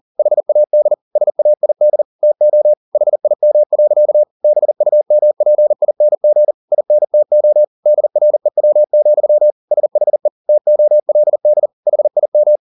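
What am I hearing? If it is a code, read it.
Morse code, heard as a keyed tone, about 40 wpm, 600 Hz. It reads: HAD SAID TO HIM. BUMPING INTO DREW, SHE TOLD HIM
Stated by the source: Morse code at 40 wpm, practice text, CW med Gnister.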